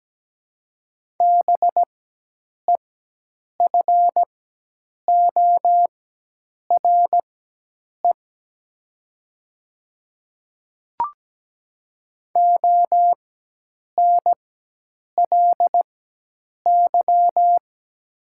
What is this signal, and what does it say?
Morse code, heard as a keyed tone, about 17 wpm, spaced as for 8 wpm, 700 Hz. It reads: BEFORE ONLY
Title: 4x Character Spacing - Top 300 Words - 17wpm